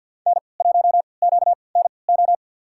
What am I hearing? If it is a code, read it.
Morse code, heard as a keyed tone, about 48 wpm, spaced as for 28 wpm, 700 Hz. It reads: N1QNO
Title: USA Call Signs - 5 Characters - 3x Character Spacing - 50wpm